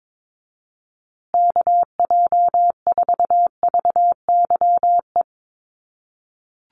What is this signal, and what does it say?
Morse code, heard as a keyed tone, about 22 wpm, 700 Hz. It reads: KJ4VYE
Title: USA Call Signs Repeated 2x 22wpm